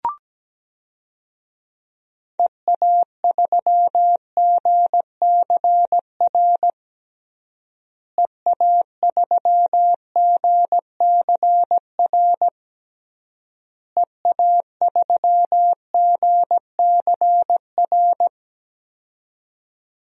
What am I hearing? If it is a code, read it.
Morse code, heard as a keyed tone, about 17 wpm, 700 Hz. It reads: EA3GCR EA3GCR EA3GCR
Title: DXCC-100 Call Signs - 6 Characters - Repeated 3x - 17wpm